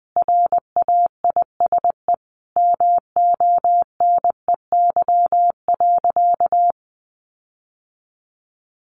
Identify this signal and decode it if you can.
Morse code, heard as a keyed tone, about 20 wpm, 700 Hz. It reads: RAISE MONEY.
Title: Sentences from Top 1400 Words 20wpm